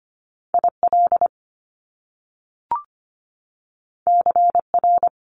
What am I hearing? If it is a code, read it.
Morse code, heard as a keyed tone, about 25 wpm, 700 Hz. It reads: IL CR
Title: Top Two Letter Combinations in Words 25wpm